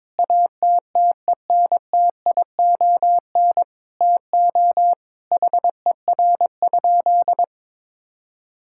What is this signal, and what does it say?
Morse code, heard as a keyed tone, about 22 wpm, 700 Hz. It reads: ATTENTION TO HER?
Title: Sentences from Top 1000 Words - Review Part 6 - 22wpm